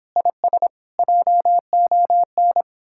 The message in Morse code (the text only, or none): IS JON